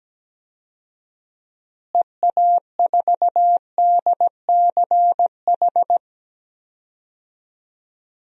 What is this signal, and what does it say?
Morse code, heard as a keyed tone, about 17 wpm, 700 Hz. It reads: EA4DCH
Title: DXCC-100 Call Signs - 6 Characters - Repeated 2x - 17wpm